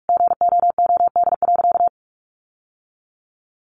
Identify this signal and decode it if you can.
Morse code, heard as a keyed tone, about 44 wpm, 700 Hz. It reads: GOOD.
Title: Sentences from Top 1100 Words 45wpm